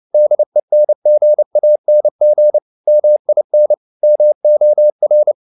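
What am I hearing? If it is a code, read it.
Morse code, heard as a keyed tone, about 29 wpm, 600 Hz. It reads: DENGANG MIN MOR
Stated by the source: Fast (QRQ) Morse at 29 wpm, from CW med Gnister.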